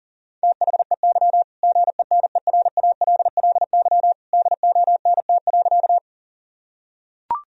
Morse code, heard as a keyed tone, about 40 wpm, 700 Hz. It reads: THEY GENERALLY DONT.
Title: Sentences from Top 1000 Words - Review Part 4 - 40wpm